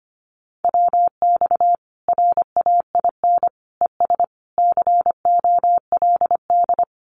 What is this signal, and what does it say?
Morse code, heard as a keyed tone, about 25 wpm, 700 Hz. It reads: WX RAIN ES COLD
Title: Common QSO Phrases 25wpm